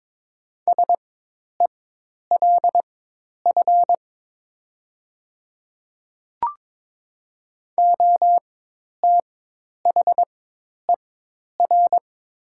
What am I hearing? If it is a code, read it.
Morse code, heard as a keyed tone, about 21 wpm, spaced as for 10 wpm, 700 Hz. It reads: SELF OTHER
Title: 4x Character Spacing - Top 200 Words - 22wpm